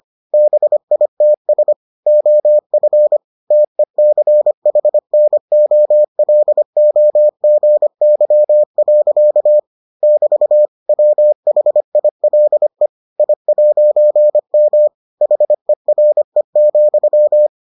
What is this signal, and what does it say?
Morse code, heard as a keyed tone, about 25 wpm, 600 Hz. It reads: BITS OF TECHNOLOGY. <BT> WHILE I'M HERE,